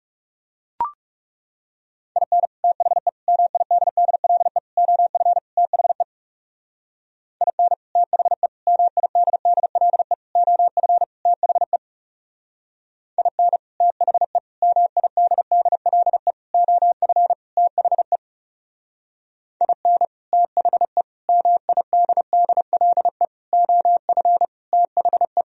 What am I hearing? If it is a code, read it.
Morse code, heard as a keyed tone, about 39 wpm, 700 Hz. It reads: IN THE MIDDLE OF THE IN THE MIDDLE OF THE IN THE MIDDLE OF THE IN THE MIDDLE OF THE